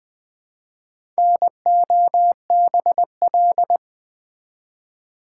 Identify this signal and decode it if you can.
Morse code, heard as a keyed tone, about 20 wpm, 700 Hz. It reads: NOBL
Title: Build A Word - 1 Letter At A Time - 5-Letter Words - 20wpm